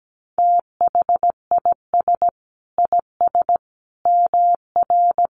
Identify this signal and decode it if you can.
Morse code, heard as a keyed tone, about 17 wpm, 700 Hz. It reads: THIS IS MR
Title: A Sound Of Thunder 17wpm